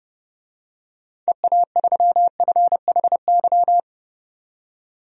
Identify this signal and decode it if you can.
Morse code, heard as a keyed tone, about 30 wpm, 700 Hz. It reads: EA3FHY
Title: DXCC-100 Call Signs - 6 Characters - 30wpm